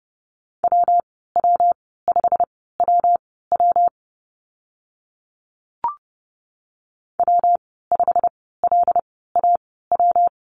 Morse code, heard as a keyed tone, about 30 wpm, spaced as for 17 wpm, 700 Hz. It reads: WW5WW W5LAW
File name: Lesson 19g - Letter w - Call Signs 30wpm